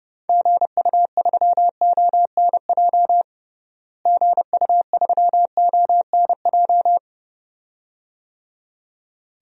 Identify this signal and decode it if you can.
Morse code, heard as a keyed tone, about 32 wpm, 700 Hz. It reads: GU3ONJ GU3ONJ
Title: DXCC-100 Call Signs - 6 Characters - Repeated 2x - 30wpm